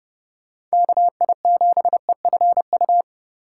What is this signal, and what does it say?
Morse code, heard as a keyed tone, about 30 wpm, 700 Hz. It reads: KI7EFU
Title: USA Call Signs - 4x Character Spacing - 30wpm